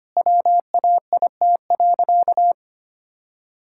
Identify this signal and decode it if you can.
Morse code, heard as a keyed tone, about 25 wpm, 700 Hz. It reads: WAIT.
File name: practice / 25wpm / Sentences with 3-Word Phrases - 25wpm